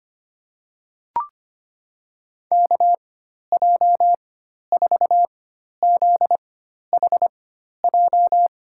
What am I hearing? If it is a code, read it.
Morse code, heard as a keyed tone, about 25 wpm, spaced as for 12 wpm, 700 Hz. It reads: KJ4ZHJ
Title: USA Call Signs - 6 Characters - 4x Character Spacing - 25wpm